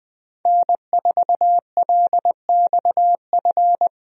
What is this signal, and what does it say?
Morse code, heard as a keyed tone, about 20 wpm, 700 Hz. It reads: N4LXF
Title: USA Call Signs - 5 Characters - Repeated 2x - 20wpm